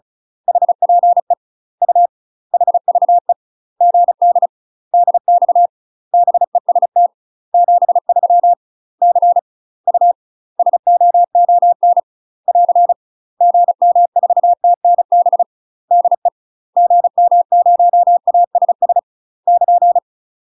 Morse code, heard as a keyed tone, about 34 wpm, 700 Hz. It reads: HPE U HVE GD DX BEST 73 C U SOON <AR> GM4TNB DE GM0ASS <KN>
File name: UK MASTER QSOs TRAINING / 35WPM / UK MASTER QSOs 0037